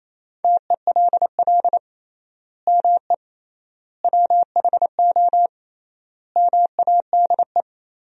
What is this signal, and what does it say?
Morse code, heard as a keyed tone, about 28 wpm, 700 Hz. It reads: TELL ME WHO MADE
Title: 3x Word-Spacing - Sentences from Top 200 Words - 28wpm